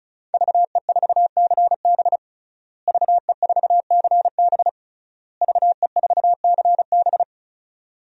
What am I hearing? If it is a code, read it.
Morse code, heard as a keyed tone, about 34 wpm, 700 Hz. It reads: VE4CB VE4CB VE4CB
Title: DXCC-100 Call Signs - 5 Characters - Repeated 3x - 35wpm